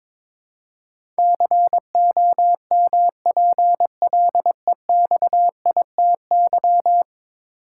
Morse code, heard as a keyed tone, about 22 wpm, 700 Hz. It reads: COMPLEXITY
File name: practice / 22wpm / 10 Letter Words - First 3 Letters Given - 22wpm